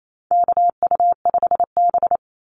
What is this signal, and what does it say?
Morse code, heard as a keyed tone, about 28 wpm, 700 Hz. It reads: KU5B
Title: Call Signs - 28wpm